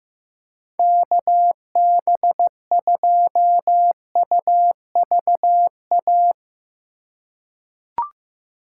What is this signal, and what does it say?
Morse code, heard as a keyed tone, about 15 wpm, 700 Hz. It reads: KB2UVA
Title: USA Call Signs - 2x Character Spacing - 15wpm